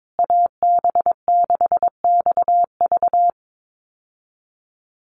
Morse code, heard as a keyed tone, about 22 wpm, 700 Hz. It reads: AB6XV
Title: USA Call Signs Repeated 2x 22wpm